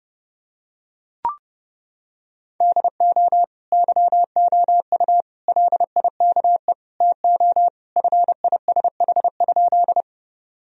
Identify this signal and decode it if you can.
Morse code, heard as a keyed tone, about 30 wpm, 700 Hz. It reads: DO YOU LIKE TO FISH?
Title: Sentences from Top 1000 Words - Review Part 1 - 30wpm